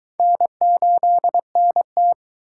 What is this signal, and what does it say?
Morse code, heard as a keyed tone, about 22 wpm, 700 Hz. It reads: N8NT